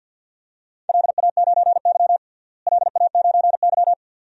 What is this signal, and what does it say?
Morse code, heard as a keyed tone, about 50 wpm, 700 Hz. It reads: LA9Y LA9Y
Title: DXCC-100 Call Signs - 4 Characters - Repeated 2x - 50wpm